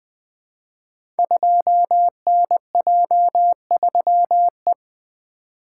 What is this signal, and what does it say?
Morse code, heard as a keyed tone, about 20 wpm, 700 Hz. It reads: 2NJ3E
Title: Code Groups - 5 Letter:Numbers - 20wpm